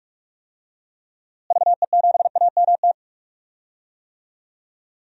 Morse code, heard as a keyed tone, about 41 wpm, 700 Hz. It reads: VE7AMT
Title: DXCC-100 Call Signs - 6 Characters - Repeated 2x - 45wpm